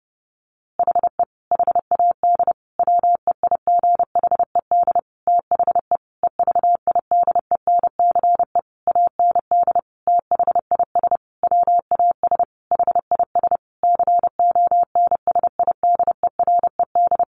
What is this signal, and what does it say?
Morse code, heard as a keyed tone, about 30 wpm, 700 Hz. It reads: HE HAD WEIGHED THE EVIDENCE AND THIS WAS HIS CONSIDERED
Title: A Sound Of Thunder 30wpm